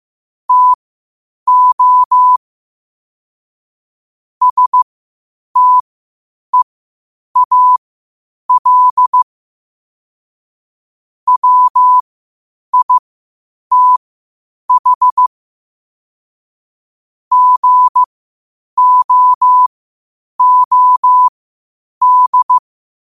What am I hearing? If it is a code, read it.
Morse code, heard as a keyed tone, about 15 wpm, spaced as for 8 wpm, 1000 Hz. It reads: TO STEAL WITH GOOD
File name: qotdfiles / 8wpm